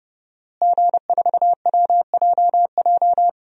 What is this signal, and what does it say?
Morse code, heard as a keyed tone, about 30 wpm, 700 Hz. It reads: G4WJJ
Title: DXCC-100 Call Signs - 5 Characters - Repeated 3x - 30wpm